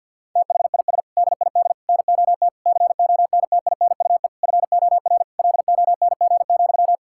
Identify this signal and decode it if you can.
Morse code, heard as a keyed tone, about 50 wpm, 700 Hz. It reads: THIS DID NOT CONTINUE FOR LONG,
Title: Sentences from Top 1000 Words - Review Part 6 - 50wpm